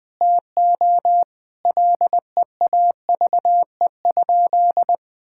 Morse code, heard as a keyed tone, about 20 wpm, 700 Hz. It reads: TO LEAVE?
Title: Sentences with 4-Word Phrases - 20wpm